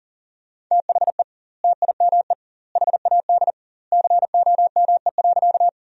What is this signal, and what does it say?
Morse code, heard as a keyed tone, about 40 wpm, 700 Hz. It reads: THE TIME HAD COME.
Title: 2x Word-Spacing - Sentences from Top 100 Words - 40wpm